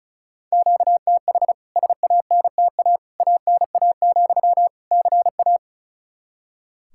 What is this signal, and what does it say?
Morse code, heard as a keyed tone, about 36 wpm, 700 Hz. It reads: QTH SANTA ANA, CA
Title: K1USN SST QSO Practice 35wpm